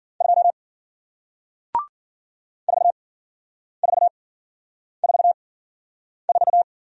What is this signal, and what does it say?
Morse code, heard as a keyed tone, about 54 wpm, spaced as for 9 wpm, 700 Hz. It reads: 2 4444